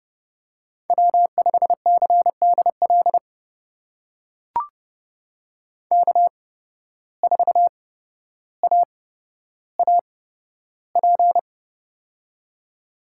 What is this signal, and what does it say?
Morse code, heard as a keyed tone, about 30 wpm, spaced as for 13 wpm, 700 Hz. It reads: W5CDL K4AAP
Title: USA Call Signs - 5 Characters - 8x Character Spacing - 30wpm